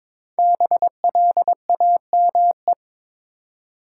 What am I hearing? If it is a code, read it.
Morse code, heard as a keyed tone, about 22 wpm, 700 Hz. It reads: BLAME
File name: 5 Letter Words - First 3 Letters Given - 22wpm